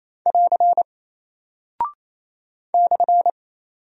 Morse code, heard as a keyed tone, about 28 wpm, 700 Hz. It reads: <AR> /
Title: Single Character (Letter:Number:Punctuation:Prosign) Alt 28wpm